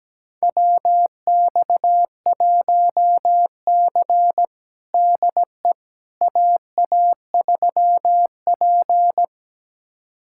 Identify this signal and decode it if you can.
Morse code, heard as a keyed tone, about 17 wpm, 700 Hz. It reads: WX1C DE AA3P
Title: USA Callsign DE Callsign - 4 Characters - 17wpm